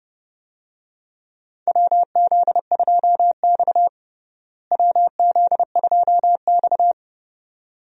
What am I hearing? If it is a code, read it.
Morse code, heard as a keyed tone, about 29 wpm, 700 Hz. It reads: WZ2X WZ2X
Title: USA Call Signs - 4 Characters - Repeated 2x - 30wpm